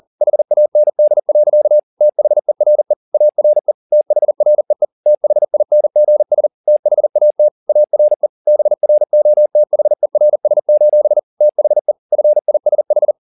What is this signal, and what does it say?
Morse code, heard as a keyed tone, about 40 wpm, 600 Hz. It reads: HAND. THERE ARE THREE THINGS THAT ARE BROTHERS: THE FISH